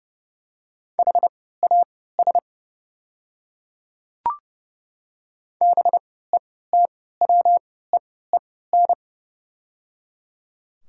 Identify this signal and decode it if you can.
Morse code, heard as a keyed tone, about 29 wpm, spaced as for 17 wpm, 690 Hz. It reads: HAS BETWEEN
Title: 3x Character Spacing - Top 300 Words - 30wpm